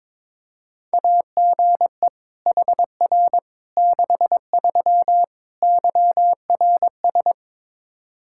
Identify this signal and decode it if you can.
Morse code, heard as a keyed tone, about 22 wpm, 700 Hz. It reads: AGE HR 63 YRS
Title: Practice QSO - Exchange 3 - More Info - 22wpm